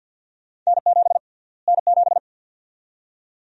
Morse code, heard as a keyed tone, about 51 wpm, 700 Hz. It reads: N7 N7
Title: USA Call Sign Prefixes - Repeated 2x - 50wpm